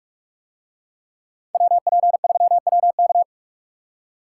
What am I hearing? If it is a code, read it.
Morse code, heard as a keyed tone, about 44 wpm, 700 Hz. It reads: WP3WK